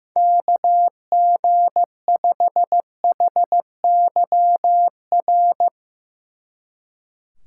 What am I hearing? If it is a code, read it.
Morse code, heard as a keyed tone, about 15 wpm, 700 Hz. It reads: KG5HYR
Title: Speed Racing - USA 2x3 Call Signs 15wpm